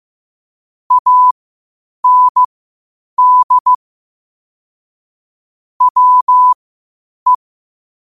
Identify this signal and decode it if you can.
Morse code, heard as a keyed tone, about 15 wpm, spaced as for 8 wpm, 1000 Hz. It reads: AND WE
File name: qotdfiles / 8wpm